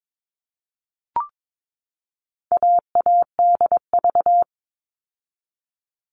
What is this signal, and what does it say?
Morse code, heard as a keyed tone, about 22 wpm, 700 Hz. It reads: AADV
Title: All Cut Numbers - 4-Digit 22wpm